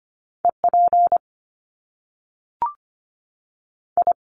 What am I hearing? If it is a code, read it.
Morse code, heard as a keyed tone, about 25 wpm, 700 Hz. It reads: E P I
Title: Top Two Letter Combinations in Words 25wpm